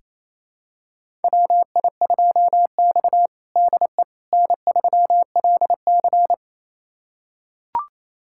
Morse code, heard as a keyed tone, about 27 wpm, 700 Hz. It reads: WI2X DE N3LC
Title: USA Callsign DE Callsign - 4 Characters - 28wpm